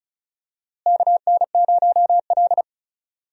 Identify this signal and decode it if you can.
Morse code, heard as a keyed tone, about 36 wpm, 700 Hz. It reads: KN0L
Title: USA Call Signs - 4 Characters - Repeated 2x - 35wpm